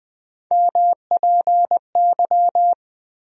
Morse code, heard as a keyed tone, about 20 wpm, 700 Hz. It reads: MPY
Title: USA Call Sign Postfixes - 3 Letters 20wpm